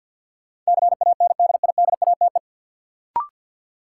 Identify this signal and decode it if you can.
Morse code, heard as a keyed tone, about 52 wpm, 700 Hz. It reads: CANDIDATE
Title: Sentences from Top 1600 Words - New Word Drill Alt - 50wpm